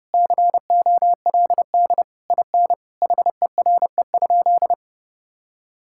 Morse code, heard as a keyed tone, about 30 wpm, 700 Hz. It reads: COLD IN HERE?
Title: Sentences from Top 1000 Words - Review Part 6 - 30wpm